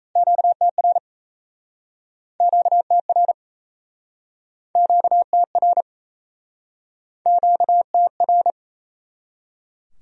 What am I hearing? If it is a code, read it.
Morse code, heard as a keyed tone, about 38 wpm, 700 Hz. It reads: QTR QTR QTR QTR